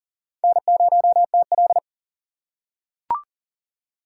Morse code, heard as a keyed tone, about 39 wpm, 700 Hz. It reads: N0TL
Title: USA Call Signs - 4x Character Spacing - 40wpm